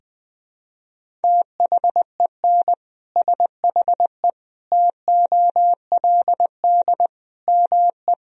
Morse code, heard as a keyed tone, about 20 wpm, 700 Hz. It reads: THEN SHE TOLD ME
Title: Sentences from Top 400 Words and Abbreviations 20wpm